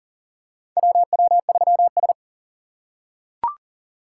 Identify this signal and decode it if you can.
Morse code, heard as a keyed tone, about 40 wpm, 690 Hz. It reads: WW3S
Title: USA Call Signs - 4 Characters - 2x Character Spacing - 40wpm